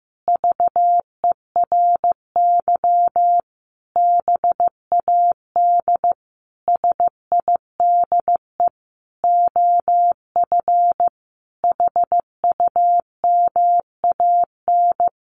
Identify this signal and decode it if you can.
Morse code, heard as a keyed tone, about 15 wpm, 700 Hz. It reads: VERY BAD SIDE OF HUMAN
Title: Sentences from Top 700 Words 15wpm